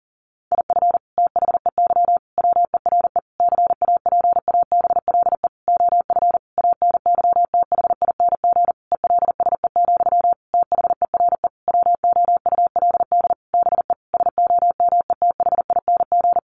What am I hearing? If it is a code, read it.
Morse code, heard as a keyed tone, about 40 wpm, 700 Hz. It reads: IF THEY WERE CAPABLE OF ANYTHING ELSE, THERE WOULD BE SOMETHING
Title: Sentences from Top 1200 Words 40wpm